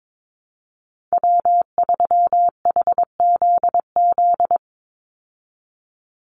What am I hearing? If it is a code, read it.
Morse code, heard as a keyed tone, about 22 wpm, 700 Hz. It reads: W3HZZ
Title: Call Signs Repeated 3x - 22wpm